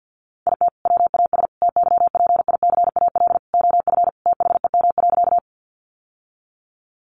Morse code, heard as a keyed tone, about 50 wpm, 700 Hz. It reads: IT WAS TYPICAL OF THEM.
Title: Sentences from Top 1100 Words 50wpm